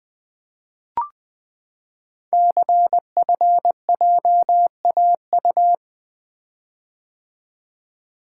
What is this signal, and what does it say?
Morse code, heard as a keyed tone, about 20 wpm, 700 Hz. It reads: CFJAU
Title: Code Groups - 5 Letters - 20wpm